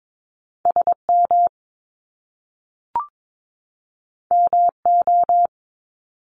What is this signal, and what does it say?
Morse code, heard as a keyed tone, about 22 wpm, 700 Hz. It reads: SM MO
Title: Top Two Letter Combinations in Words 22wpm